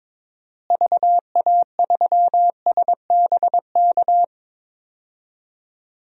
Morse code, heard as a keyed tone, about 22 wpm, 700 Hz. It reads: VA3SBK
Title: DXCC-100 Call Signs - 6 Characters - Repeated 2x - 22wpm